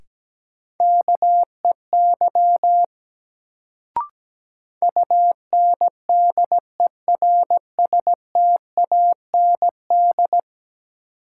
Morse code, heard as a keyed tone, about 17 wpm, 700 Hz. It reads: KEY UNDERSTAND
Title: Sentences from Top 600 Words - New Word Drill - 17wpm